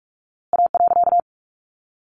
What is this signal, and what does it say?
Morse code, heard as a keyed tone, about 45 wpm, 700 Hz. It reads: A.
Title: A Sound Of Thunder 45wpm